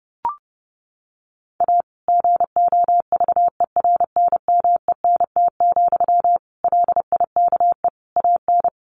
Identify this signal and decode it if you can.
Morse code, heard as a keyed tone, about 30 wpm, 700 Hz. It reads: A GOVERNMENT, LIKE AN
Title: Sentences from Top 1400 Words 30wpm